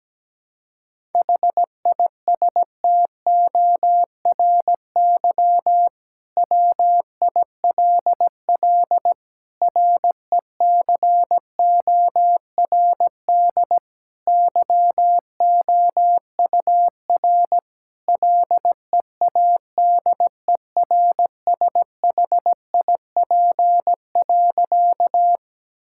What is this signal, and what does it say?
Morse code, heard as a keyed tone, about 17 wpm, 700 Hz. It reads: HISTORY WILL RECORD YOUR LEADERSHIP.
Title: Sentences from Top 1600 Words 17wpm